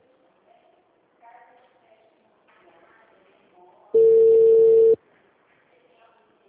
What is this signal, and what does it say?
Ringback tone (China, 450 Hz), with speech.
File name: out- -2015- - - .4588